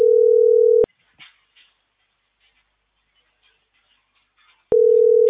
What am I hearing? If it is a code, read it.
Ringback tone (North America), 440 + 480 Hz.